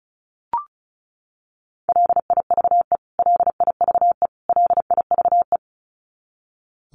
Morse code, heard as a keyed tone, about 34 wpm, 700 Hz. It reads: LIVE LIVE LIVE
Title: Top 200 Words Repeated 35wpm